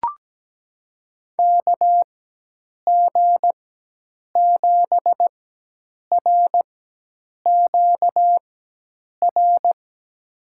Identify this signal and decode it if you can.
Morse code, heard as a keyed tone, about 17 wpm, spaced as for 8 wpm, 700 Hz. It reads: KG7RQR
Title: USA Call Signs - 4x Character Spacing - 17wpm